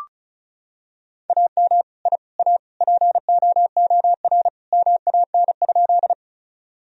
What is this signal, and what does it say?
Morse code, heard as a keyed tone, about 36 wpm, 700 Hz. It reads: AM I A POOR MAN?
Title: Sentences from Top 1000 Words - Review Part 2 - 35wpm